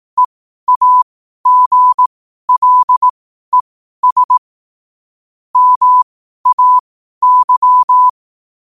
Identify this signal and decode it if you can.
Morse code, heard as a keyed tone, about 18 wpm, spaced as for 12 wpm, 1000 Hz. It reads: EAGLES MAY
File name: qotdfiles / 12wpm